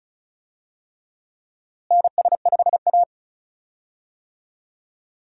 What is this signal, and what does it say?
Morse code, heard as a keyed tone, about 36 wpm, 700 Hz. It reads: NS5A
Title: USA Call Signs - 4 Characters - Repeated 3x - 35wpm